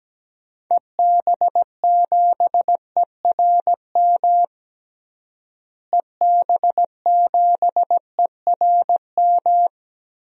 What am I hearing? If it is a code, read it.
Morse code, heard as a keyed tone, about 17 wpm, 700 Hz. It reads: EB7ERM EB7ERM
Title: DXCC-100 Call Signs - 6 Characters - Repeated 2x - 17wpm